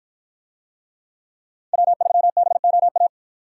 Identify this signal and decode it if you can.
Morse code, heard as a keyed tone, about 49 wpm, 700 Hz. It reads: W3BOA